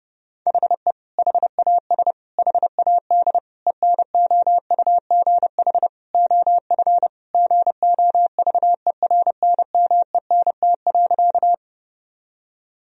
Morse code, heard as a keyed tone, about 32 wpm, 700 Hz. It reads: HE HAS HAD ENOUGH OF GOVERNMENT.
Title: Sentences from Top 1000 Words - Review Part 3 - 30wpm